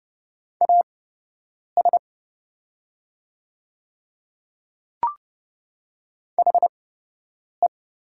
Morse code, heard as a keyed tone, about 30 wpm, spaced as for 8 wpm, 700 Hz. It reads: AS HE
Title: 8x Character Spacing - Top 100 Words - 30wpm